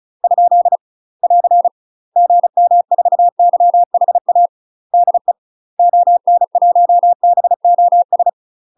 Morse code, heard as a keyed tone, about 33 wpm, 700 Hz. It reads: ? <AR> GM4YHA DE ON1BOS
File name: UK MASTER QSOs TRAINING / 35WPM / UK MASTER QSOs 0012